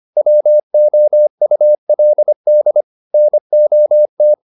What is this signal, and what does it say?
Morse code, heard as a keyed tone, about 25 wpm, 600 Hz. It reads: WOULD NOT
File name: the old man and the sea part 1 25wpm